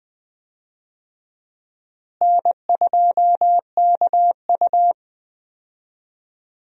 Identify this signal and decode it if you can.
Morse code, heard as a keyed tone, about 21 wpm, 700 Hz. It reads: N2KU
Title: USA Call Signs - 4 Characters - Repeated 2x - 20wpm